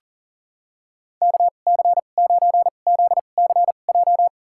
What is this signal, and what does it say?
Morse code, heard as a keyed tone, about 38 wpm, spaced as for 28 wpm, 700 Hz. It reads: KC9ZCJ
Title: USA Call Signs - 6 Characters - 2x Character Spacing - 40wpm